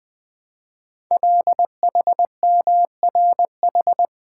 Morse code, heard as a keyed tone, about 20 wpm, 700 Hz. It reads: LHMRH
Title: Code Groups - 5 Letters - 20wpm